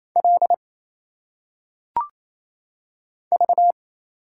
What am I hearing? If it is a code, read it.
Morse code, heard as a keyed tone, about 27 wpm, 700 Hz. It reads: L V
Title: Single Letters - Phonetic Pronunciation 28wpm